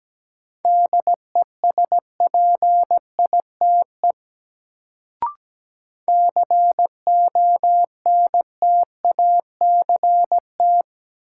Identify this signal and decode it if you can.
Morse code, heard as a keyed tone, about 17 wpm, 700 Hz. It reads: DESPITE CONTACT
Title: Sentences from Top 1000 Words - Review Part 7 - New Word Drill - 17wpm